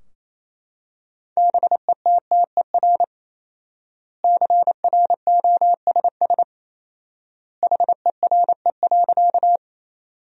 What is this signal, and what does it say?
Morse code, heard as a keyed tone, about 28 wpm, 700 Hz. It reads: BETTER CROSS HERE.
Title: 4x Word-Spacing - Sentences from Top 400 Words - 28wpm